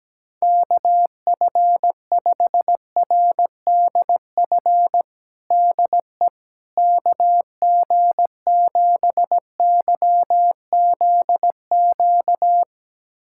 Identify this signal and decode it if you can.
Morse code, heard as a keyed tone, about 17 wpm, 700 Hz. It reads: KF5RDF DE KG7YZQ
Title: USA Callsign DE Callsign - 6 Characters - 17wpm